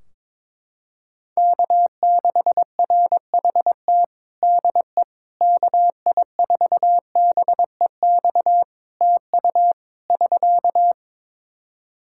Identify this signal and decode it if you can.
Morse code, heard as a keyed tone, about 22 wpm, 700 Hz. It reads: K6RHT DE KI4BEX TU <SK>
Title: K1USN SST QSO Practice 22wpm